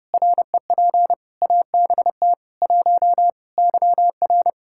Morse code, heard as a keyed tone, about 30 wpm, 700 Hz. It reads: REP ABT 1 YR
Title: Practice QSO - Exchange 3 - More Info - 30wpm